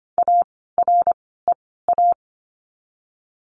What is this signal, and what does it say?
Morse code, heard as a keyed tone, about 25 wpm, 700 Hz. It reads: A R E A
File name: course / 25:10wpm / 094 Lesson 16f - Letter c - Words 25wpm